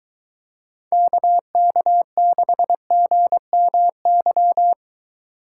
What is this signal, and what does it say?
Morse code, heard as a keyed tone, about 22 wpm, 700 Hz. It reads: KK6GMY